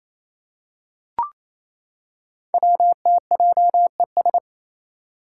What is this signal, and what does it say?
Morse code, heard as a keyed tone, about 27 wpm, 700 Hz. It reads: WTJES
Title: Code Groups - 5 Letters - 28wpm